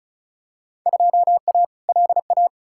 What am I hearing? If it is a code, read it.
Morse code, heard as a keyed tone, about 34 wpm, 700 Hz. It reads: 2A LA
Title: Field Day Exchanges - 35wpm